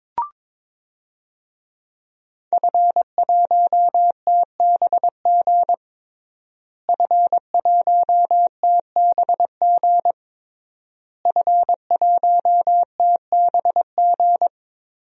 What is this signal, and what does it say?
Morse code, heard as a keyed tone, about 22 wpm, 700 Hz. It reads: F1TBG F1TBG F1TBG
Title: DXCC-100 Call Signs - 5 Characters - Repeated 3x - 22wpm